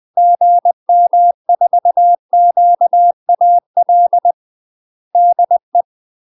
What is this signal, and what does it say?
Morse code, heard as a keyed tone, about 20 wpm, 700 Hz. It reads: GM4QAL DE
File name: UK MASTER QSOs TRAINING/20WPM